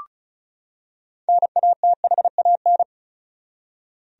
Morse code, heard as a keyed tone, about 34 wpm, 700 Hz. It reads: NATHAN
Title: USA Common Names - 35wpm